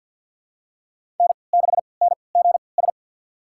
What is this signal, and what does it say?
Morse code, heard as a keyed tone, about 52 wpm, spaced as for 28 wpm, 690 Hz. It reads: N6NGS
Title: USA Call Signs - 5 Characters - 3x Character Spacing - 50wpm